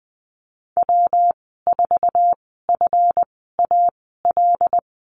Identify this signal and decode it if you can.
Morse code, heard as a keyed tone, about 20 wpm, spaced as for 14 wpm, 700 Hz. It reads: W4FAL
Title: 140 Lesson 23g - Letter f - Call Signs 20wpm